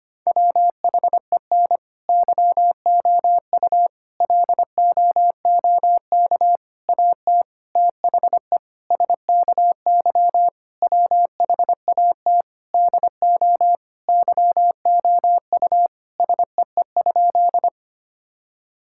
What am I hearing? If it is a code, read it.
Morse code, heard as a keyed tone, about 25 wpm, 700 Hz. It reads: WHEN YOU LOOK AT THE SKY WHAT DO YOU SEE?
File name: practice / 25wpm / Sentences with 3-Word Phrases - 25wpm